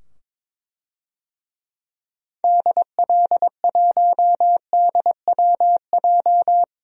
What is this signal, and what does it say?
Morse code, heard as a keyed tone, about 22 wpm, 700 Hz. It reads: DL1DWJ
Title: DXCC-100 Call Signs - 6 Characters - 22wpm